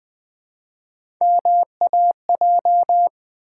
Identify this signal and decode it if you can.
Morse code, heard as a keyed tone, about 20 wpm, 700 Hz. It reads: MAJ
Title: Build A Word - 1 Letter At A Time - 3 to 5-Letter Words - 20wpm